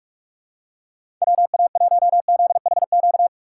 Morse code, heard as a keyed tone, about 47 wpm, 700 Hz. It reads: WA1ZHQ
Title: USA Call Signs - 6 Characters - 4x Character Spacing - 45wpm